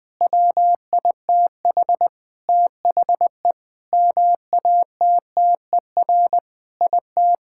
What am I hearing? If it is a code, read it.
Morse code, heard as a keyed tone, about 21 wpm, 700 Hz. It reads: WITH THE MATTER IT
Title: Sentences with 7-Word Phrases - Spoken Beginning - 20wpm